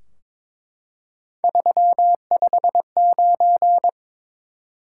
Morse code, heard as a keyed tone, about 22 wpm, 700 Hz. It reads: 359
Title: Top 180 QSO Elements 22wpm